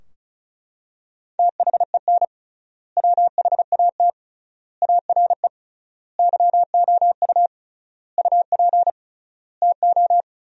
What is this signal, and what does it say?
Morse code, heard as a keyed tone, about 36 wpm, 700 Hz. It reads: THEN WHAT ARE YOU UP TO
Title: 3x Word-Spacing - Sentences from Top 100 Words - 35wpm